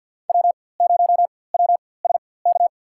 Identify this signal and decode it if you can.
Morse code, heard as a keyed tone, about 50 wpm, spaced as for 23 wpm, 700 Hz. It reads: W0WSK
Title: USA Call Signs - 4x Character Spacing - 50wpm